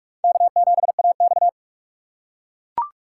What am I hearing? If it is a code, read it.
Morse code, heard as a keyed tone, about 44 wpm, 700 Hz. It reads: K7AX